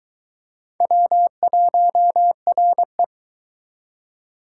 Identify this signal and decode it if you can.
Morse code, heard as a keyed tone, about 23 wpm, 700 Hz. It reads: W1RE